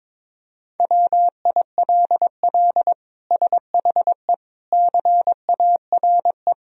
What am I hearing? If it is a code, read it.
Morse code, heard as a keyed tone, about 22 wpm, 700 Hz. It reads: WILL SHE CARE
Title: Sentences from Top 1000 Words - Review Part 3 - 22wpm